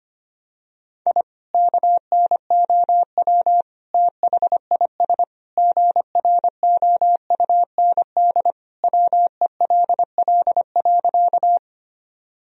Morse code, heard as a keyed tone, about 25 wpm, 700 Hz. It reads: I KNOW THIS GROUND WELL.
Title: Sentences from Top 1000 Words - Review Part 2 - 25wpm